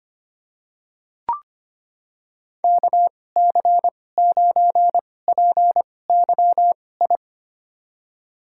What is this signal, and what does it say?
Morse code, heard as a keyed tone, about 25 wpm, spaced as for 18 wpm, 700 Hz. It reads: KC9PYI
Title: USA Call Signs - 6 Characters - 2x Character Spacing - 25wpm